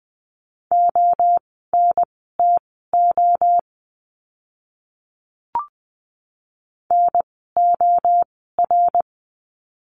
Morse code, heard as a keyed tone, about 20 wpm, spaced as for 14 wpm, 700 Hz. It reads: ONTO NOR
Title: 054 Lesson 10f - Letter r - Words 20wpm